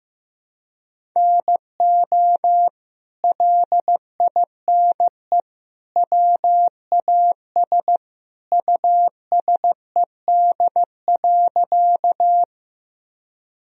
Morse code, heard as a keyed tone, about 15 wpm, 700 Hz. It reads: NO LINE WAS USED.
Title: Sentences from Top 1000 Words - Review Part 1 - 15wpm